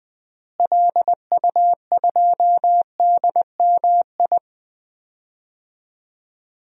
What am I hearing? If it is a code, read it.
Morse code, heard as a keyed tone, about 20 wpm, 700 Hz. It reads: LU2DMI
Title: DXCC-100 Call Signs - 6 Characters - Repeated 3x - 20wpm